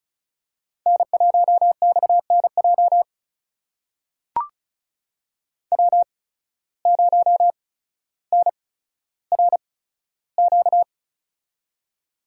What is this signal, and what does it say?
Morse code, heard as a keyed tone, about 36 wpm, spaced as for 15 wpm, 700 Hz. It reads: N1XNJ W0NRQ
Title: USA Call Signs - 5 Characters - 8x Character Spacing - 35wpm